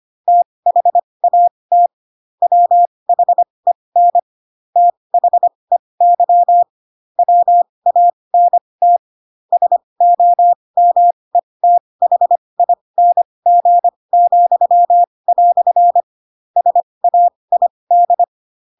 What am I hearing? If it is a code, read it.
Morse code, heard as a keyed tone, about 25 wpm, spaced as for 20 wpm, 700 Hz. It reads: THAT WHEN THEY WANT SOMETHING," SAID
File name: Pinocchio / Pinocchio ch8 15 25 ik1hge